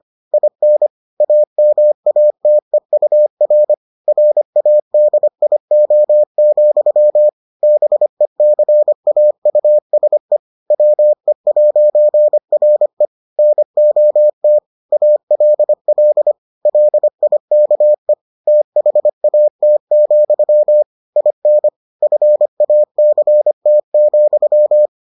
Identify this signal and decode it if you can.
Morse code, heard as a keyed tone, about 25 wpm, 600 Hz. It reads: IN AMATEUR RADIO, BECAUSE WE'RE NOT ALL LIKE THAT, IN FACT,